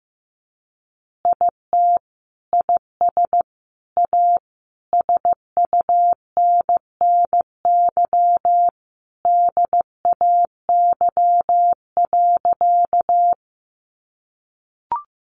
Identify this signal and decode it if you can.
Morse code, heard as a keyed tone, about 15 wpm, 700 Hz. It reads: IT IS A SUNNY DAY.
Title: Additional Easy Phrases 15wpm